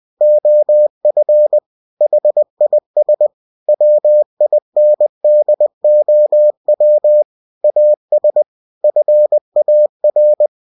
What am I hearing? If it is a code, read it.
Morse code, heard as a keyed tone, about 20 wpm, 600 Hz. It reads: OF HIS WINDOW AS FAR
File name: cw-books / winnie-the-pooh / winnie-20wpm-page-0092